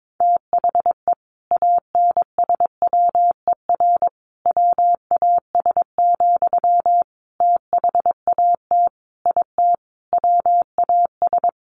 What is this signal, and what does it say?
Morse code, heard as a keyed tone, about 22 wpm, 700 Hz. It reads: THE ANSWER WAS, THAT IT WAS
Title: Sentences from Top 1100 Words 22wpm